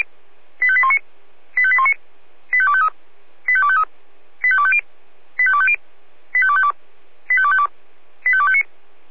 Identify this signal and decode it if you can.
Five-tone selective call (ZVEI), heard as quick runs of five tones, tones of about 70 ms. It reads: 85819 85819 85253 85253 85250 85250 85252 85252 85259